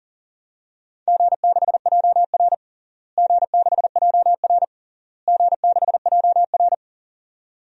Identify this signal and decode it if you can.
Morse code, heard as a keyed tone, about 40 wpm, 700 Hz. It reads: G6JR G6JR G6JR
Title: DXCC-100 Call Signs - 4 Characters - Repeated 3x - 40wpm